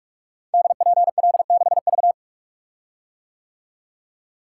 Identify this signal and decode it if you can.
Morse code, heard as a keyed tone, about 46 wpm, 700 Hz. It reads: DPL6V
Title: Code Groups - 5 Letter:Numbers - 45wpm